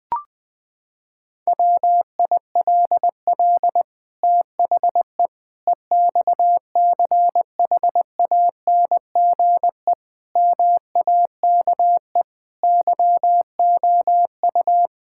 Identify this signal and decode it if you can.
Morse code, heard as a keyed tone, about 20 wpm, 700 Hz. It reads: WILL THE EXCHANGE MAKE YOU
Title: Sentences from Top 1000 Words - Review Part 6 - 20wpm